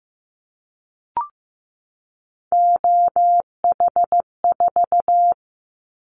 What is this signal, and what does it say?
Morse code, heard as a keyed tone, about 15 wpm, 700 Hz. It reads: OH4
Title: Callsign Prefixes for Top 50 DXCC - 15wpm